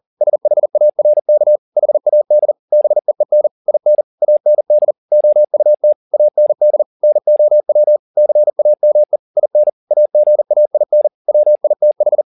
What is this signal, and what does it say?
Morse code, heard as a keyed tone, about 40 wpm, 600 Hz. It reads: SHARK HAD BEEN IN AND OUT AND NOW CAME IN AGAIN WITH
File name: the old man and the sea 2 40wpm